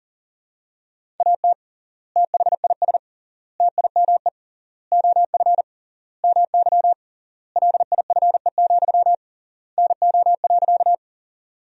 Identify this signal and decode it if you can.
Morse code, heard as a keyed tone, about 40 wpm, 700 Hz. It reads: AT THIS TIME OF MY LIFE, NO.
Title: 3x Word-Spacing - Sentences from Top 300 Words - 40wpm